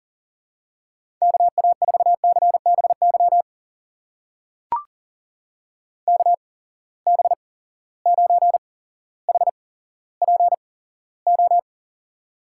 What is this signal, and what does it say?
Morse code, heard as a keyed tone, about 40 wpm, spaced as for 17 wpm, 700 Hz. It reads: KA4CBY KB9HPO
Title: USA Call Signs - 6 Characters - 8x Character Spacing - 40wpm